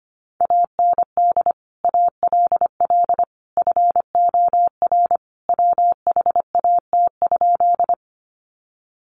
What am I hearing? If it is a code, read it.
Morse code, heard as a keyed tone, about 25 wpm, 700 Hz. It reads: AND ALL FOR WHAT?
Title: Sentences from Top 100 Words 25wpm